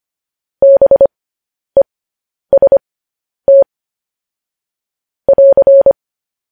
Morse code, heard as a keyed tone, about 25 wpm, spaced as for 10 wpm, 550 Hz. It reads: BEST <AR>
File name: podcast / News headlines-1x-25x10